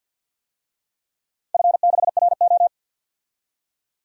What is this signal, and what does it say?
Morse code, heard as a keyed tone, about 50 wpm, 700 Hz. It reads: F6RO
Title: DXCC-100 Call Signs - 4 Characters - 50wpm